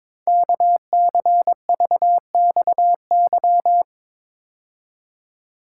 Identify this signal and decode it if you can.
Morse code, heard as a keyed tone, about 23 wpm, 700 Hz. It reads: KCVXY
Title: Code Groups - 5 Letters - 22wpm